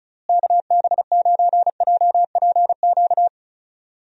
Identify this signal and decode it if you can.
Morse code, heard as a keyed tone, about 34 wpm, 700 Hz. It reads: KB9JPQ
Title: USA Call Signs - 6 Characters - Repeated 3x - 35wpm